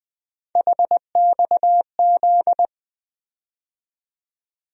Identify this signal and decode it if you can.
Morse code, heard as a keyed tone, about 20 wpm, 700 Hz. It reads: HXZ
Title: USA Call Sign Postfixes - 3 Letters 20wpm